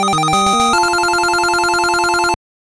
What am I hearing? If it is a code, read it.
There is no code.